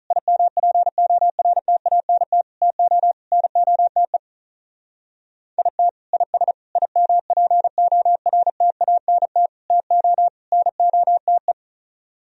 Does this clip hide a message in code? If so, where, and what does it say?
Yes: Morse code from 0.1 to 11.5 s, heard as a keyed tone, about 44 wpm, 700 Hz. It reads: IMPORTANT TO NOTE IT IS IMPORTANT TO NOTE